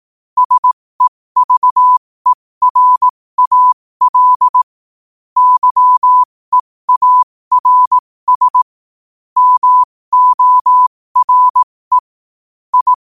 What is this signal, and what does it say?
Morse code, heard as a keyed tone, about 18 wpm, spaced as for 15 wpm, 1000 Hz. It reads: SEVERAL YEARS MORE I
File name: qotdfiles / 15wpm